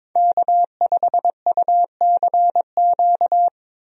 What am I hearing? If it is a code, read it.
Morse code, heard as a keyed tone, about 22 wpm, 700 Hz. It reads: K5UCQ
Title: USA Call Signs - 5 Characters - Repeated 3x - 22wpm